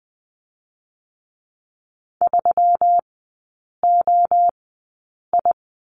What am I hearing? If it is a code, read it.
Morse code, heard as a keyed tone, about 20 wpm, spaced as for 8 wpm, 700 Hz. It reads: 3OI
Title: 184 Lesson 30e - Letter b - Three Characters 20wpm